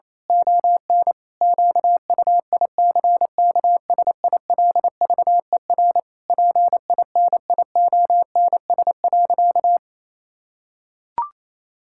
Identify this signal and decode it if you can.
Morse code, heard as a keyed tone, about 28 wpm, 700 Hz. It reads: ON QUICKSILVER PINIONS.
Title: Pangram Challenge v2 - 28wpm